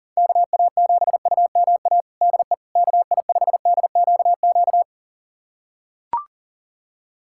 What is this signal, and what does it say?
Morse code, heard as a keyed tone, about 40 wpm, 700 Hz. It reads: KA7UMA DE KI5DQQ
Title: USA Callsign DE Callsign - 6 Characters - 40wpm